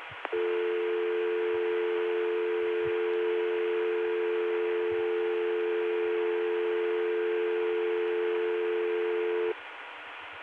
Dial tone (North America), 350 + 440 Hz.